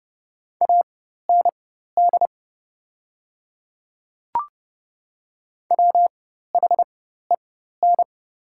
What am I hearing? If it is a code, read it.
Morse code, heard as a keyed tone, about 30 wpm, spaced as for 14 wpm, 700 Hz. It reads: AND WHEN
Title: 4x Character Spacing - Top 300 Words - 30wpm